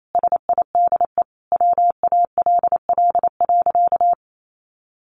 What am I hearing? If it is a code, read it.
Morse code, heard as a keyed tone, about 28 wpm, 700 Hz. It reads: SIDE WALL.
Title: Sentences from Top 1100 Words 28wpm